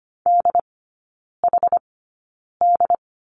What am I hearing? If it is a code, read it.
Morse code, heard as a keyed tone, about 25 wpm, spaced as for 9 wpm, 690 Hz. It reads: DHD
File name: Lesson 15e - Number 5 - Three Characters 25wpm